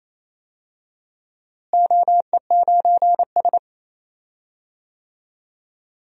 Morse code, heard as a keyed tone, about 28 wpm, 700 Hz. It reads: OE9S
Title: DXCC-100 Call Signs - 4 Characters - Repeated 3x - 28wpm